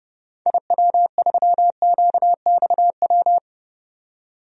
Morse code, heard as a keyed tone, about 30 wpm, 700 Hz. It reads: IW3QXW
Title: DXCC-100 Call Signs - 6 Characters - 30wpm